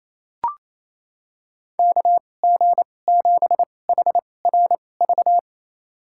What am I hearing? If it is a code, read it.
Morse code, heard as a keyed tone, about 28 wpm, spaced as for 20 wpm, 700 Hz. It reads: KG7HRV
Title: USA Call Signs - 2x Character Spacing - 28wpm